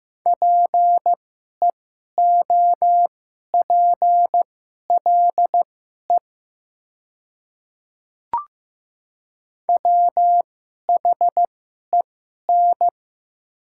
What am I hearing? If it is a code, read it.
Morse code, heard as a keyed tone, about 15 wpm, spaced as for 11 wpm, 700 Hz. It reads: PEOPLE WHEN
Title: 2x Character Spacing - Top 100 Words - 15wpm